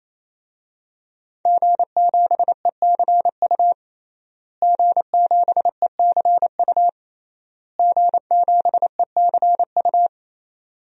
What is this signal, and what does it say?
Morse code, heard as a keyed tone, about 28 wpm, 700 Hz. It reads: G7ECU G7ECU G7ECU
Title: DXCC-100 Call Signs - 5 Characters - Repeated 3x - 28wpm